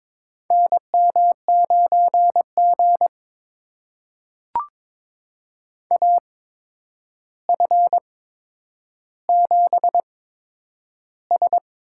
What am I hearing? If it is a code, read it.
Morse code, heard as a keyed tone, about 22 wpm, spaced as for 9 wpm, 700 Hz. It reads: NM9G AF7S
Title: USA Call Signs - 4 Characters - 8x Character Spacing - 22wpm